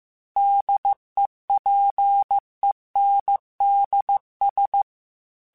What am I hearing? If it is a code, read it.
Morse code, heard as a keyed tone, about 15 wpm, 800 Hz. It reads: DEPENDS
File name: Technical Subjects / Technical Subjects-mp3s / Sw Pwr Sup @13